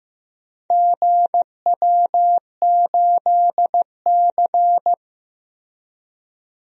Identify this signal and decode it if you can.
Morse code, heard as a keyed tone, about 15 wpm, 700 Hz. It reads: GW8C
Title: DXCC-100 Call Signs - 4 Characters - Repeated 2x - 15wpm